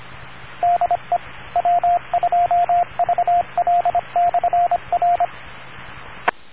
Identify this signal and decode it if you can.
Morse code, heard as a keyed tone, about 25 wpm, 700 Hz, with radio noise. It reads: DE W2VL/R E